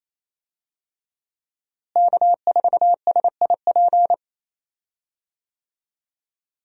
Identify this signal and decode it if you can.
Morse code, heard as a keyed tone, about 28 wpm, 700 Hz. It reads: K4SIP
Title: USA Call Signs - 5 Characters - Repeated 2x - 28wpm